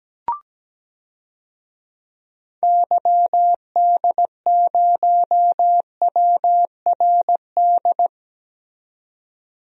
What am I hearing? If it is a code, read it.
Morse code, heard as a keyed tone, about 17 wpm, 700 Hz. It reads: YD0WRD
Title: DXCC-100 Call Signs - 6 Characters - 17wpm